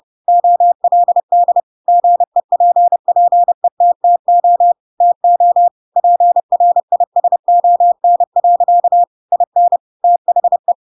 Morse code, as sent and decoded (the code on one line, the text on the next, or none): --- .-.. -.. / --. .--. .--. - - --- / - --- / .--. .-. .. ... --- -. .-.-.- / .. -. / - .... .
OLD GEPPETTO TO PRISON. IN THE